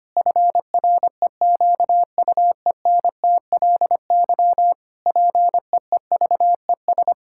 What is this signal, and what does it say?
Morse code, heard as a keyed tone, about 25 wpm, 700 Hz. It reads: FREQUENTLY PEEVES
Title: Pangram Challenge v2 - 25wpm